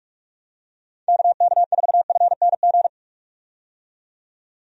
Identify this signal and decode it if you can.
Morse code, heard as a keyed tone, about 46 wpm, 700 Hz. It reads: KK4FNG
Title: USA Call Signs - 6 Characters - 45wpm